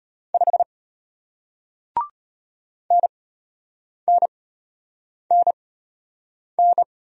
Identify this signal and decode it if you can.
Morse code, heard as a keyed tone, about 33 wpm, spaced as for 8 wpm, 700 Hz. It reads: 5 NNNN